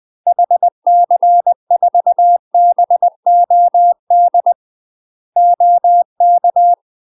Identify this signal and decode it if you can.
Morse code, heard as a keyed tone, about 20 wpm, 700 Hz. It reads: HC4BOD OK
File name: UK MASTER QSOs TRAINING / 20WPM / UK MASTER QSOs 0013